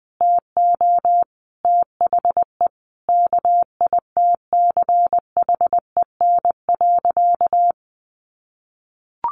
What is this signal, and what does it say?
Morse code, heard as a keyed tone, about 20 wpm, 700 Hz. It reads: TO THE KITCHEN.
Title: Sentences from Top 1400 Words 20wpm